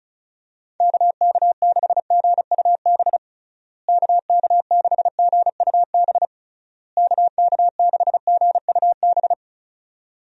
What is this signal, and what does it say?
Morse code, heard as a keyed tone, about 34 wpm, 700 Hz. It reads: KK6GUB KK6GUB KK6GUB
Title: USA Call Signs - 6 Characters - Repeated 3x - 35wpm